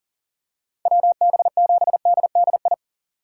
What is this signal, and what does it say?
Morse code, heard as a keyed tone, about 39 wpm, 700 Hz. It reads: WB7DDI
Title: USA Call Signs - 6 Characters - 6x Character Spacing - 40wpm